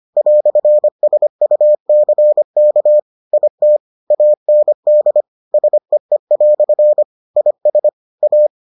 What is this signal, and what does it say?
Morse code, heard as a keyed tone, about 25 wpm, 600 Hz. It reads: "SUCK IT AND SEE" IS A